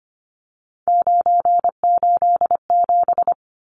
Morse code, heard as a keyed tone, about 25 wpm, 700 Hz. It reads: 987
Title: Numbers - 3-Digit 25wpm